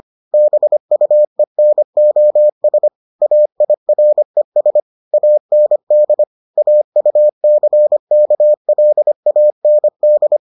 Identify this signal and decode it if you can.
Morse code, heard as a keyed tone, about 25 wpm, 600 Hz. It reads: BUENOS AIRES AND AUCKLAND